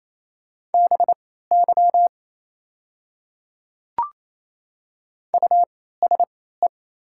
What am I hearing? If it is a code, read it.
Morse code, heard as a keyed tone, about 28 wpm, spaced as for 16 wpm, 700 Hz. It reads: BY USE
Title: 3x Character Spacing - Top 100 Words - 28wpm